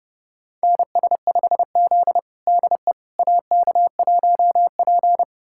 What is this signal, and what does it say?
Morse code, heard as a keyed tone, about 30 wpm, 700 Hz. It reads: NS5Z DE AK1P
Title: USA Callsign DE Callsign - 4 Characters - 30wpm